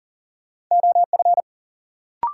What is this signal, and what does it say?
Morse code, heard as a keyed tone, about 38 wpm, 700 Hz. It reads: OF
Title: Sentences from Top 100 Words - New Word Drill - 40wpm